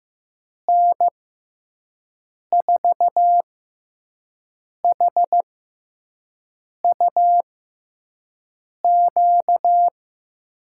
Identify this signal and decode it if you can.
Morse code, heard as a keyed tone, about 15 wpm, spaced as for 5 wpm, 700 Hz. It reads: N4HUQ
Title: USA Call Signs - 6 Characters - 6x Character Spacing - 15wpm